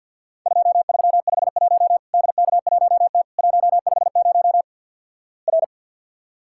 Morse code, heard as a keyed tone, about 50 wpm, 700 Hz. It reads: 2351 DK1T 150 R
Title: Worked All Europe DX Contest - QTC Exchange with Repeats - 50wpm